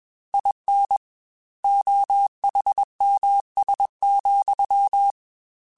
Morse code, heard as a keyed tone, about 21 wpm, 800 Hz. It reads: IN OHMS,
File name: Technical Subjects / Technical Subjects-mp3s / XC @20